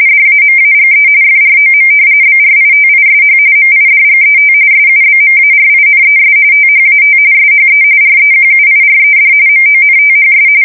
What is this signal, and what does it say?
RTTY, about 45.45 baud. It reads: DAMAGE FROM BRITISH MIDGET SUBMARINES THAT SHE REQUIRED MONTHS
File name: fsk170 / battleshipnj